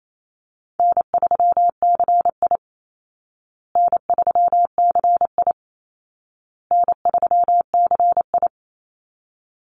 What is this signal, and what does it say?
Morse code, heard as a keyed tone, about 28 wpm, 700 Hz. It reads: N3CI N3CI N3CI
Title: Call Signs Repeated 2x - 28wpm